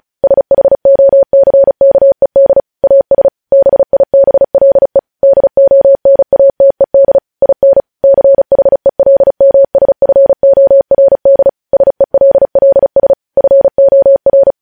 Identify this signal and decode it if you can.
Morse code, heard as a keyed tone, about 36 wpm, 550 Hz. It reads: SHOCKED AS BIBLE DONATED IN CHELMSFORD SELLS FOR
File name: podcast / News headlines-1x-35x35